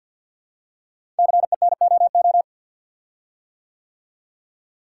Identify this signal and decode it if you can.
Morse code, heard as a keyed tone, about 50 wpm, 700 Hz. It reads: CENOO
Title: Code Groups - 5 Letters - 50wpm